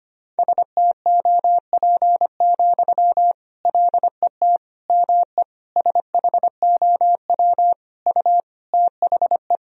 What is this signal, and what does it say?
Morse code, heard as a keyed tone, about 25 wpm, 700 Hz. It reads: STOP, LET ME SHOW U THE
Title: Sentences from Top 300 Words and Abbreviations 25wpm